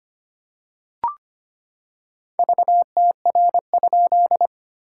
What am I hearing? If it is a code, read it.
Morse code, heard as a keyed tone, about 25 wpm, 700 Hz. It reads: VTR?
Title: SOTA QSO Practice 25wpm